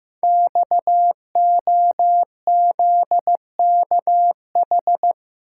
Morse code, heard as a keyed tone, about 15 wpm, 700 Hz. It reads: XOZKH
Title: Code Groups - 5 Letter:Numbers - 15wpm